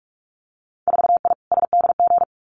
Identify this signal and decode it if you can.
Morse code, heard as a keyed tone, about 46 wpm, 690 Hz. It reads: 4I SDG
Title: Winter Field Day Exchanges Encore 45wpm